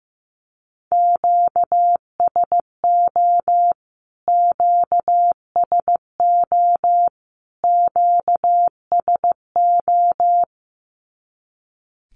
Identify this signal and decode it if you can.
Morse code, heard as a keyed tone, about 15 wpm, 700 Hz. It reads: QSO QSO QSO
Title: Q-Codes Repeated 3x 15wpm